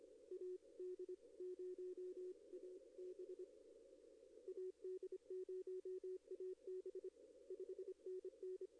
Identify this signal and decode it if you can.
Morse code, heard as a keyed tone, about 25 wpm, 370 Hz, with radio noise. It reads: AD0AB AD0AB 5NN
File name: contests / arrl-160-k5tr-op-w5tn -1300